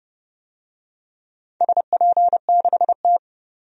Morse code, heard as a keyed tone, about 29 wpm, 700 Hz. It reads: SP6T
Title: DXCC-100 Call Signs - 4 Characters - 30wpm